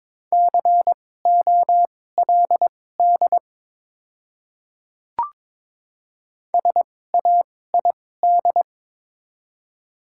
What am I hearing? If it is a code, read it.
Morse code, heard as a keyed tone, about 22 wpm, spaced as for 16 wpm, 700 Hz. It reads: COLD SAID
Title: 2x Character Spacing - Top 500 Words - 22wpm